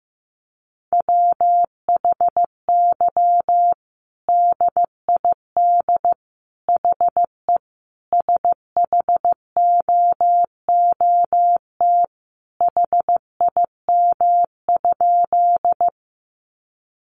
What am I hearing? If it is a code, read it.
Morse code, heard as a keyed tone, about 15 wpm, 700 Hz. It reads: WHY DID HE SHOOT HIM?
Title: Sentences from Top 1200 Words 15wpm